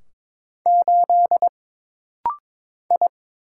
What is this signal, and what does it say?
Morse code, heard as a keyed tone, about 22 wpm, 700 Hz. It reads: 8 I
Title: Single Letter-Number - Rapid-Fire 22wpm